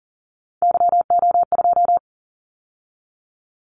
Morse code, heard as a keyed tone, about 40 wpm, 700 Hz. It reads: YO2
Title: Callsign Prefixes for Top 25 DXCC - 40wpm